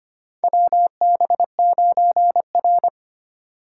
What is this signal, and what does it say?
Morse code, heard as a keyed tone, about 25 wpm, 700 Hz. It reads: WB9R
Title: USA Call Signs - 4 Characters - 2x Character Spacing - 25wpm